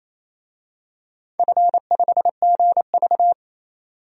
Morse code, heard as a keyed tone, about 28 wpm, 700 Hz. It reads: F5GV
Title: DXCC-100 Call Signs - 4 Characters - 28wpm